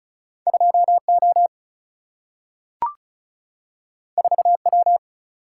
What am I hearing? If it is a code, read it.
Morse code, heard as a keyed tone, about 36 wpm, 700 Hz. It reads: 2 O 4 W
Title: Single-Letter Single-Number 35wpm